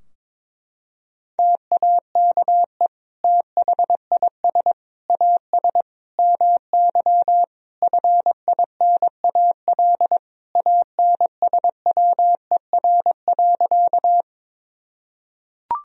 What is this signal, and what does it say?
Morse code, heard as a keyed tone, about 22 wpm, 700 Hz. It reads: TAKE THIS AS MY FINAL ANSWER.
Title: Sentences from Top 1000 Words - Review Part 2 - 22wpm